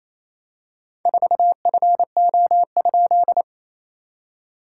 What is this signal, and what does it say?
Morse code, heard as a keyed tone, about 28 wpm, 700 Hz. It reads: 4FO?
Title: SOTA QSO Practice 28wpm